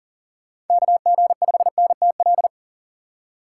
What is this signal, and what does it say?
Morse code, heard as a keyed tone, about 40 wpm, 700 Hz. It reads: KG5NTL
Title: USA Call Signs - 6 Characters - 4x Character Spacing - 40wpm